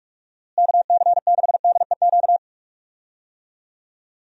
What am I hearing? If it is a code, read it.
Morse code, heard as a keyed tone, about 46 wpm, 700 Hz. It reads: KC6DEQ
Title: USA Call Signs - 6 Characters - Repeated 2x - 45wpm